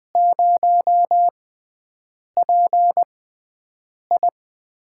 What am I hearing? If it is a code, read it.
Morse code, heard as a keyed tone, about 20 wpm, spaced as for 7 wpm, 700 Hz. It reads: 0PI
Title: USA Call Signs - 5 Characters - 6x Character Spacing - 20wpm